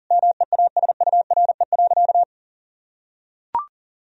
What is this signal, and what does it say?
Morse code, heard as a keyed tone, about 40 wpm, 700 Hz. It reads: MEASURE.
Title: Sentences from Top 1000 Words - Review Part 1 - 40wpm